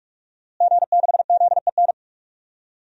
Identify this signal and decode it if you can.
Morse code, heard as a keyed tone, about 44 wpm, 700 Hz. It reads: G6ZEN
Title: DXCC-100 Call Signs - 5 Characters - Repeated 3x - 45wpm